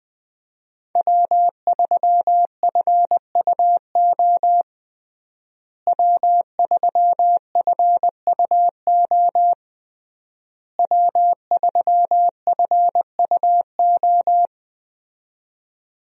Morse code, heard as a keyed tone, about 20 wpm, 700 Hz. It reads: W3FUO W3FUO W3FUO
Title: USA Call Signs - 5 Characters - Repeated 3x - 20wpm